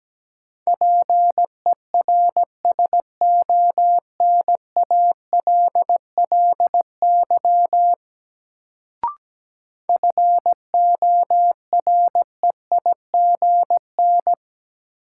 Morse code, heard as a keyed tone, about 17 wpm, 700 Hz. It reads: PERSONALLY FOREIGN
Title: Sentences from Top 1400 Words - New Word Drill - 17wpm